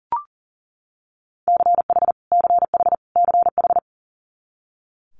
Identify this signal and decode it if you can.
Morse code, heard as a keyed tone, about 40 wpm, 700 Hz. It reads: CH CH CH
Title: Multi-letter Phonemes Repeated 3x 40wpm